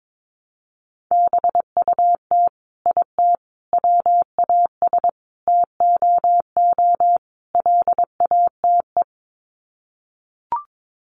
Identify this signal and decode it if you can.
Morse code, heard as a keyed tone, about 22 wpm, 700 Hz. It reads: BUT IT WAS TOO LATE
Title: Sets of 5 Words Encore2 22wpm